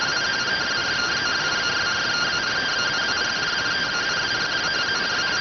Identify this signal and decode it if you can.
Two-tone teleprinter signal, about 170 Hz shift.